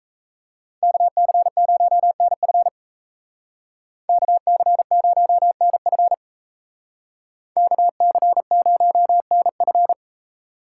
Morse code, heard as a keyed tone, about 36 wpm, 700 Hz. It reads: KC0NF KC0NF KC0NF